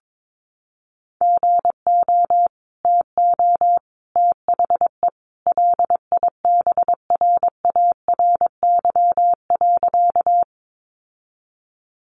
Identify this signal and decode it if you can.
Morse code, heard as a keyed tone, about 22 wpm, 700 Hz. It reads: GO TO THE LIBRARY.
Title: Sentences from Top 600 Words 22wpm